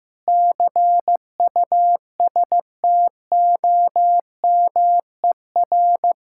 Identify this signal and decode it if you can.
Morse code, heard as a keyed tone, about 15 wpm, 700 Hz. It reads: CUSTOMER
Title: Practice QSO - Exchange 3 - More Info - 15wpm